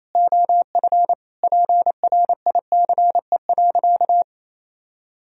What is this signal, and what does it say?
Morse code, heard as a keyed tone, about 28 wpm, 700 Hz. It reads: OF PRICE.
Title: Sentences from Top 1000 Words - Review Part 4 - 28wpm